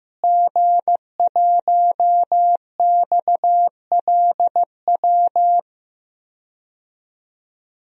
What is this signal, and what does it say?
Morse code, heard as a keyed tone, about 15 wpm, 700 Hz. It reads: G1XLW
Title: DXCC-100 Call Signs - 5 Characters - Repeated 2x - 15wpm